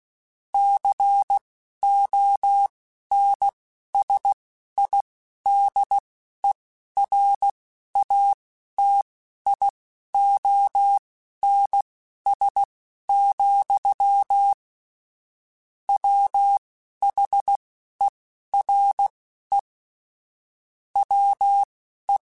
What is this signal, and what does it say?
Morse code, heard as a keyed tone, about 16 wpm, spaced as for 11 wpm, 800 Hz. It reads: CONSIDERATIONS, WHERE WE
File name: Technical Subjects / Technical Subjects-mp3s / Sw Pwr Sup @10